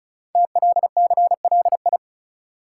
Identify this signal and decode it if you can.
Morse code, heard as a keyed tone, about 36 wpm, 700 Hz. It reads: TLCLI
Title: Code Groups - 5 Letter:Numbers - 35wpm